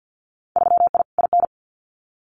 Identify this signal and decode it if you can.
Morse code, heard as a keyed tone, about 50 wpm, 700 Hz. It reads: 3I IN
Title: Winter Field Day Exchanges 50wpm